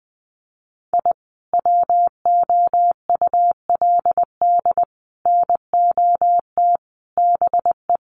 Morse code, heard as a keyed tone, about 20 wpm, 700 Hz. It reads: I WOULD NOT BE
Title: Sets of 5 Words Encore2 20wpm